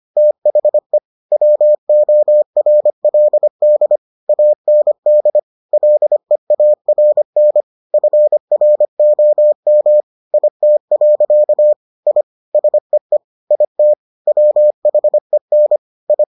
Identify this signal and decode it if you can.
Morse code, heard as a keyed tone, about 25 wpm, 600 Hz. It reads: THE WORLD AND LEARN FROM IT. I SEE IT WHEN I